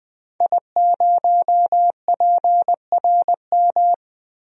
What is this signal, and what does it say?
Morse code, heard as a keyed tone, about 20 wpm, 700 Hz. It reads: I0PRM
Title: DXCC-100 Call Signs - 5 Characters - Repeated 3x - 20wpm